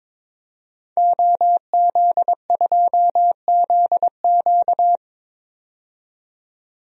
Morse code, heard as a keyed tone, about 22 wpm, 700 Hz. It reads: OZ2ZQ
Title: DXCC-100 Call Signs - 5 Characters - 22wpm